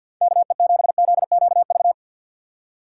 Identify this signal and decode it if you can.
Morse code, heard as a keyed tone, about 51 wpm, 700 Hz. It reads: KE7ZQV
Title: USA Call Signs - 6 Characters - 4x Character Spacing - 50wpm